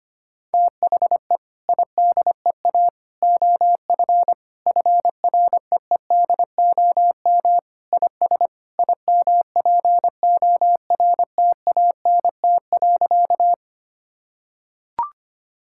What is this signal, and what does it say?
Morse code, heard as a keyed tone, about 25 wpm, 700 Hz. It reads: THE IDEA OF FREEDOM IS IMPORTANT.
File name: practice / 25wpm / Sentences with 3-Word Phrases - 25wpm